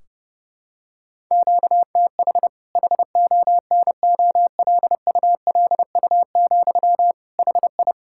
Morse code, heard as a keyed tone, about 32 wpm, 700 Hz. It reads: QTH HONOLULU, HI
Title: Practice QSO - Exchange 1 - Beginning - 30wpm